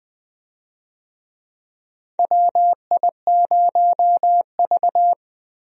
Morse code, heard as a keyed tone, about 20 wpm, 700 Hz. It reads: WI0V
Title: USA Call Signs - 4 Characters - Repeated 2x - 20wpm